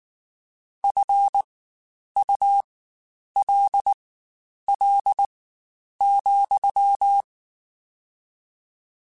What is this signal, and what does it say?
Morse code, heard as a keyed tone, about 19 wpm, spaced as for 9 wpm, 800 Hz. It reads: FULL,
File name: Spaced / Chuckles Spaced-mp3s / HalfFull sp @18